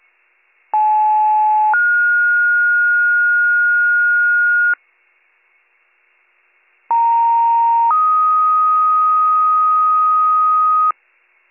Two-tone page, heard as a one-second tone, then a three-second tone, about 860 then 1430 Hz, about 930 then 1250 Hz. It is heard from 0.7 to 10.9 s.